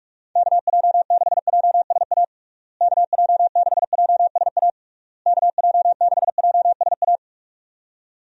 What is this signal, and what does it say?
Morse code, heard as a keyed tone, about 46 wpm, 700 Hz. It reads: KJ6JSA KJ6JSA KJ6JSA
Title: USA Call Signs - 6 Characters - Repeated 3x - 45wpm